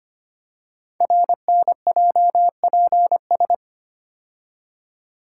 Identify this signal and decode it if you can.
Morse code, heard as a keyed tone, about 25 wpm, 700 Hz. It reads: RNJPS
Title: Code Groups - 5 Letters - 25wpm